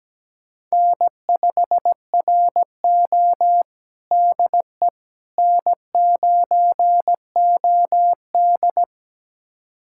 Morse code, heard as a keyed tone, about 17 wpm, 700 Hz. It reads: N5RO DE N9OD
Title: USA Callsign DE Callsign - 4 Characters - 17wpm